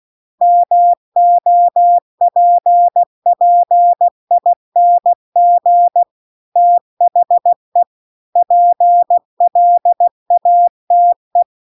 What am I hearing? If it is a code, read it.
Morse code, heard as a keyed tone, about 16 wpm, 700 Hz. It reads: MOPPING THE PLATE